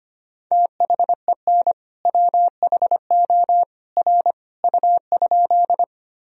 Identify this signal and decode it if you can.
Morse code, heard as a keyed tone, about 25 wpm, 700 Hz. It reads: THEN WHO R U?
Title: Sentences from Top 100 Words and Abbreviations 25wpm